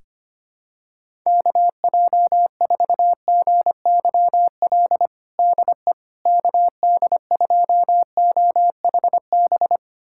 Morse code, heard as a keyed tone, about 25 wpm, 700 Hz. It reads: KJ4GYL DE KD2OHB
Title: USA Callsign DE Callsign - 6 Characters - 25wpm